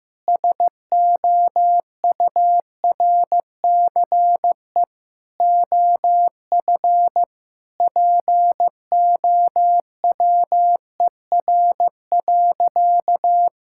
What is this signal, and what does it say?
Morse code, heard as a keyed tone, about 15 wpm, 700 Hz. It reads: SOURCE OF POWER.
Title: Sentences from Top 1000 Words - Review Part 5 - 15wpm